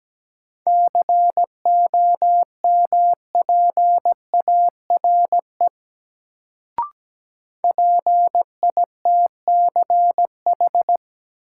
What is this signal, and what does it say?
Morse code, heard as a keyed tone, about 17 wpm, 700 Hz. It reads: COMPARE PITCH
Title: Sentences from Top 1300 Words - New Word Drill - 17wpm